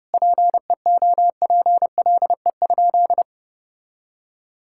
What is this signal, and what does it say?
Morse code, heard as a keyed tone, about 30 wpm, 700 Hz. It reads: PEOPLE?
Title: Sentences from Top 1000 Words - Review Part 5 - 30wpm